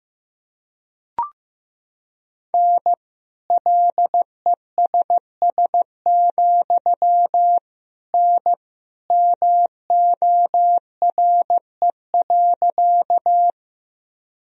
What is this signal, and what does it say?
Morse code, heard as a keyed tone, about 15 wpm, 700 Hz. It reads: N LESS, N MORE.
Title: Sentences from Top 400 Words and Abbreviations 15wpm